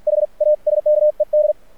Morse code, heard as a keyed tone, about 48 wpm, spaced as for 40 wpm, 600 Hz. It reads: HASKED
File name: archived / CRSnet